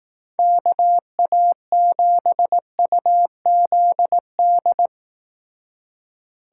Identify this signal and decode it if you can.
Morse code, heard as a keyed tone, about 18 wpm, 700 Hz. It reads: KA7UZD
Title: Speed Racing - USA 2x3 Call Signs 15wpm